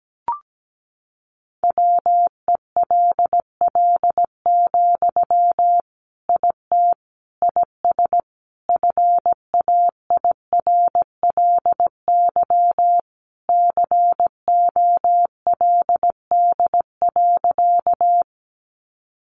Morse code, heard as a keyed tone, about 17 wpm, 700 Hz. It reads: WELL, IT IS FAIRLY COLD.
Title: Sentences from Top 1100 Words 17wpm